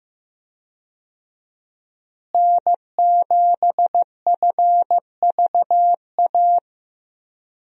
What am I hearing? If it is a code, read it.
Morse code, heard as a keyed tone, about 15 wpm, 700 Hz. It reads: N7FVA
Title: USA Call Signs - 5 Characters - Repeated 2x - 15wpm